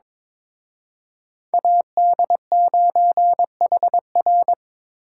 Morse code, heard as a keyed tone, about 26 wpm, 700 Hz. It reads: AD9HR
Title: Speed Racing - USA 2x2 Call Signs 22wpm